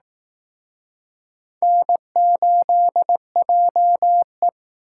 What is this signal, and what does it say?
Morse code, heard as a keyed tone, about 18 wpm, 700 Hz. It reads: N8JE
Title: Speed Racing - USA 1x2 Call Signs 15wpm